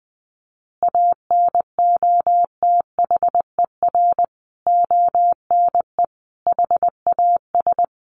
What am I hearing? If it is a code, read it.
Morse code, heard as a keyed tone, about 20 wpm, 700 Hz. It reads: ANOTHER ONE HAS
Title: Additional Easy Phrases 20wpm